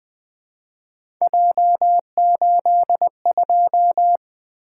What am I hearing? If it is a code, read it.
Morse code, heard as a keyed tone, about 20 wpm, 700 Hz. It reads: J82
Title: Three Letter-Number Combinations - 20wpm